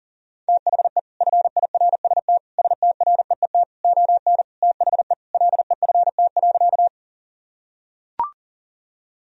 Morse code, heard as a keyed tone, about 40 wpm, 700 Hz. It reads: THE FIRST STREET ON THE LEFT.
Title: Sentences from Top 1000 Words - Review Part 3 - 40wpm